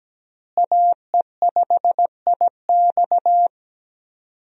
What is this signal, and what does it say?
Morse code, heard as a keyed tone, about 17 wpm, 700 Hz. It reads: AE5IX
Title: USA Call Signs - 5 Characters - Repeated 3x - 17wpm